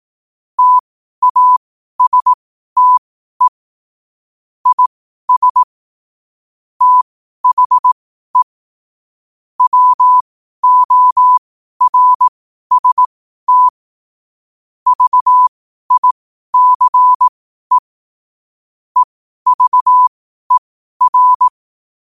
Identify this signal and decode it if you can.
Morse code, heard as a keyed tone, about 18 wpm, spaced as for 12 wpm, 1000 Hz. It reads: TASTE IS THE WORST VICE EVER
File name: qotdfiles / 12wpm